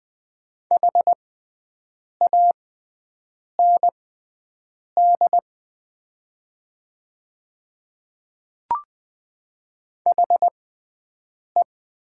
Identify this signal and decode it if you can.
Morse code, heard as a keyed tone, about 20 wpm, spaced as for 7 wpm, 700 Hz. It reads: HAND HE
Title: 6x Character Spacing - Top 300 Words - 20wpm